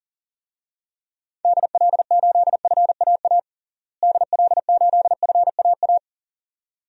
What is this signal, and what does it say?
Morse code, heard as a keyed tone, about 40 wpm, 700 Hz. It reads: DL8FAA DL8FAA
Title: DXCC-100 Call Signs - 6 Characters - Repeated 2x - 40wpm